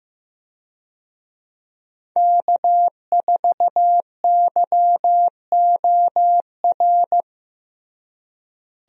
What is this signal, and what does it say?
Morse code, heard as a keyed tone, about 15 wpm, 700 Hz. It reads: K4YOR
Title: USA Call Signs - 5 Characters - Repeated 2x - 15wpm